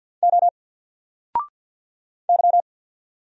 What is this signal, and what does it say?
Morse code, heard as a keyed tone, about 48 wpm, 700 Hz. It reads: O Y
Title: Single Character (Letter:Number:Punctuation:Prosign) 50wpm